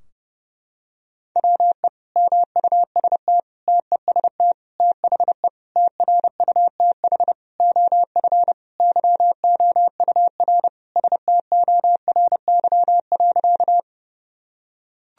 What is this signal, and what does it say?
Morse code, heard as a keyed tone, about 30 wpm, 700 Hz. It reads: WE MUST TEST THE TRUTH OF YOUR STORY.
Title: Sentences from Top 1000 Words - Review Part 3 - 30wpm